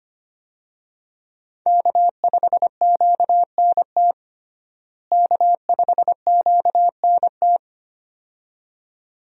Morse code, heard as a keyed tone, about 25 wpm, 700 Hz. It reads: K5QNT K5QNT
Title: USA Call Signs - 5 Characters - Repeated 2x - 25wpm